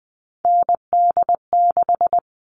Morse code, heard as a keyed tone, about 20 wpm, 700 Hz. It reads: ND6
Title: All Cut Numbers - 3-Digit 20wpm